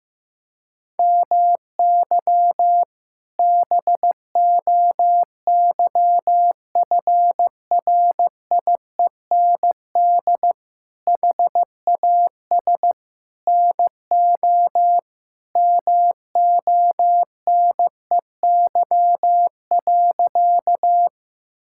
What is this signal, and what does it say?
Morse code, heard as a keyed tone, about 15 wpm, 700 Hz. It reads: MY BOYFRIEND HAS NO MONEY.
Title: Sentences from Top 1000 Words - Review Part 5 - 15wpm